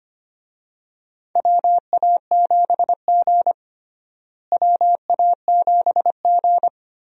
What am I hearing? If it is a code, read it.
Morse code, heard as a keyed tone, about 25 wpm, 700 Hz. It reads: WA7G WA7G
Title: USA Call Signs - 4 Characters - Repeated 2x - 25wpm